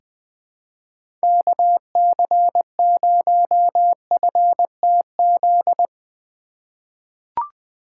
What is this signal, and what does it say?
Morse code, heard as a keyed tone, about 20 wpm, 700 Hz. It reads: KC0FTZ
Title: USA Call Signs - 6 Characters - 3x Character Spacing - 20wpm